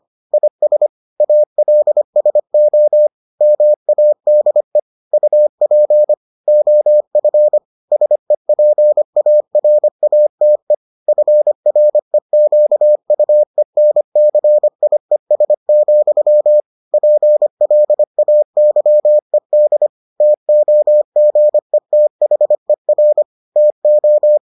Morse code, heard as a keyed tone, about 25 wpm, 600 Hz. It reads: IS ALSO MADE UP OF SEPARATE FREQUENCIES, PLAYED TOGETHER TO